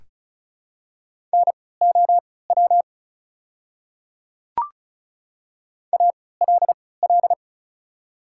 Morse code, heard as a keyed tone, about 34 wpm, spaced as for 20 wpm, 690 Hz. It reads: NOW ALL
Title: 3x Character Spacing - Top 500 Words - 35wpm